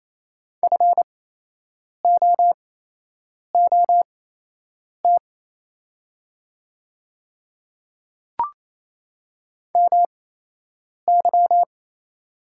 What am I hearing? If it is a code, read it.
Morse code, heard as a keyed tone, about 27 wpm, spaced as for 8 wpm, 700 Hz. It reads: FOOT MY
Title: 8x Character Spacing - Top 500 Words - 28wpm